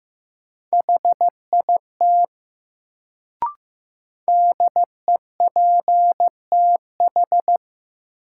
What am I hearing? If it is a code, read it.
Morse code, heard as a keyed tone, about 16 wpm, 700 Hz. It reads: HIT DEPTH
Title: Sentences from Top 1200 Words - New Word Drill Alt - 15wpm